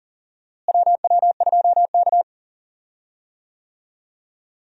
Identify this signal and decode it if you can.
Morse code, heard as a keyed tone, about 40 wpm, 700 Hz. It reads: WW2K
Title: USA Call Signs - 4 Characters - Repeated 3x - 40wpm